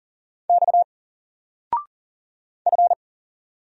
Morse code, heard as a keyed tone, about 40 wpm, 700 Hz. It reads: X F
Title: Single Letter-Number 40wpm